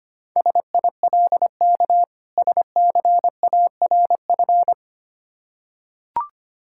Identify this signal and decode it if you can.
Morse code, heard as a keyed tone, about 25 wpm, 690 Hz. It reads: SILK SCARF
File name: practice / 25wpm / Difficult Words with Hints - 25wpm